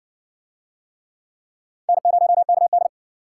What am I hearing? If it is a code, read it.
Morse code, heard as a keyed tone, about 60 wpm, 700 Hz. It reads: N0KD